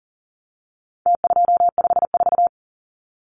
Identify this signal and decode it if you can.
Morse code, heard as a keyed tone, about 40 wpm, 700 Hz. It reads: T254
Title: Numbers - Contest Style - 4-Digit 40wpm